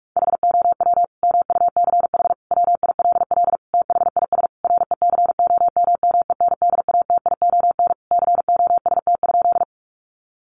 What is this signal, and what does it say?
Morse code, heard as a keyed tone, about 44 wpm, 700 Hz. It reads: HOW MUCH WILL THIS RECOMMENDATION COST?
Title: Sentences from Top 1400 Words 45wpm